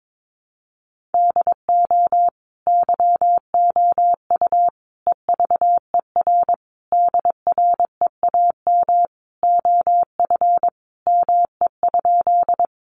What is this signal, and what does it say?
Morse code, heard as a keyed tone, about 22 wpm, 700 Hz. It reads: DO YOU EVER DREAM OF ME?
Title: Sentences from Top 1500 Words 22wpm